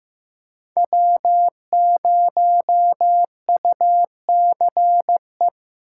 Morse code, heard as a keyed tone, about 15 wpm, 700 Hz. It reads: W0UCE
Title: K1USN SST QSO Practice - 15wpm